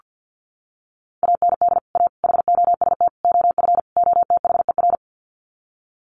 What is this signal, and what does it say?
Morse code, heard as a keyed tone, about 49 wpm, 700 Hz. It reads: AND A HOST OF OTHER
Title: Sets of 5 Words Encore2 50wpm